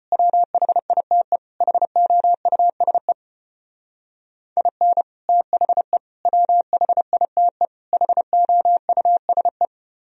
Morse code, heard as a keyed tone, about 33 wpm, 700 Hz. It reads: WHITE HOUSE IN THE WHITE HOUSE